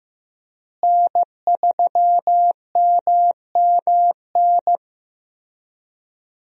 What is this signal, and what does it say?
Morse code, heard as a keyed tone, about 15 wpm, 700 Hz. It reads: N3MMN
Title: USA Call Signs - 5 Characters - Repeated 3x - 15wpm